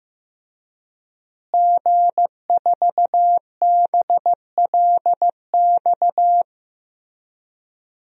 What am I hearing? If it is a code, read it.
Morse code, heard as a keyed tone, about 15 wpm, 700 Hz. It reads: G4BLX
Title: DXCC-100 Call Signs - 5 Characters - Repeated 2x - 15wpm